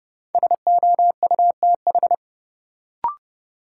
Morse code, heard as a keyed tone, about 30 wpm, 700 Hz. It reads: SOUTH
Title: Sentences from Top 400 Words - New Word Drill - 30wpm